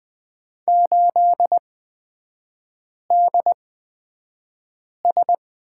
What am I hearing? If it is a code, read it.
Morse code, heard as a keyed tone, about 20 wpm, spaced as for 5 wpm, 690 Hz. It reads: 8DS
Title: 229 Lesson 37e - Letter x - Three Characters 20wpm